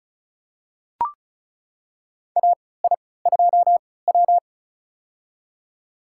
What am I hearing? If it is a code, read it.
Morse code, heard as a keyed tone, about 36 wpm, spaced as for 20 wpm, 700 Hz. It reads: AI2W
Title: USA Call Signs - 4 Characters - 3x Character Spacing - 35wpm